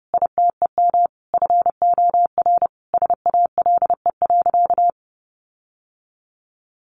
Morse code, heard as a keyed tone, about 28 wpm, 700 Hz. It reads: ITEM FOR SALE.
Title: Sentences from Top 1400 Words 30wpm